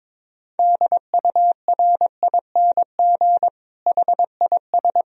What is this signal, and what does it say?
Morse code, heard as a keyed tone, about 22 wpm, 700 Hz. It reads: DURING HIS
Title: Sentences from Top 1000 Words - Review Part 2 - 22wpm